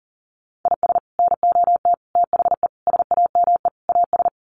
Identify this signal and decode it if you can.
Morse code, heard as a keyed tone, about 40 wpm, 700 Hz. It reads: IS NOT THE SAME AS
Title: Sets of 5 Words Encore 40wpm